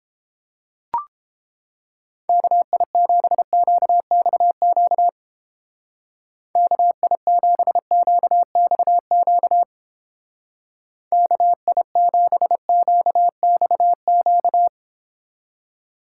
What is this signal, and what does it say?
Morse code, heard as a keyed tone, about 29 wpm, 700 Hz. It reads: KI7QXQ KI7QXQ KI7QXQ